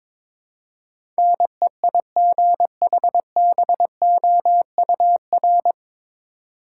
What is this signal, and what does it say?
Morse code, heard as a keyed tone, about 22 wpm, 700 Hz. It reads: NEIGHBOUR
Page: Morse Code Ninja practice 22wpm